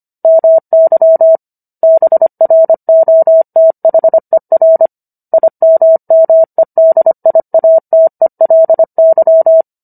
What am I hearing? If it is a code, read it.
Morse code, heard as a keyed tone, about 25 wpm, 650 Hz. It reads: MY BROTHER IMMEDIATELY